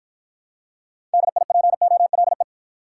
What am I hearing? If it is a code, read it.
Morse code, heard as a keyed tone, about 52 wpm, 700 Hz. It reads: DIPOLE